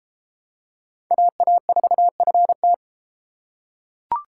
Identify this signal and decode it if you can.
Morse code, heard as a keyed tone, about 30 wpm, 700 Hz. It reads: AA4FT